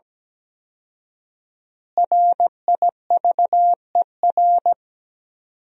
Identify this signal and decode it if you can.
Morse code, heard as a keyed tone, about 17 wpm, 700 Hz. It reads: RIVER
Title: 4x Word-Spacing - Sentences from Top 300 Words - 17wpm